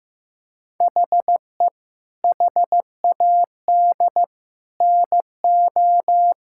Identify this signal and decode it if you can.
Morse code, heard as a keyed tone, about 16 wpm, 700 Hz. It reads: HE HAD NO
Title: Sentences from Top 1000 Words - Review Part 6 - 15wpm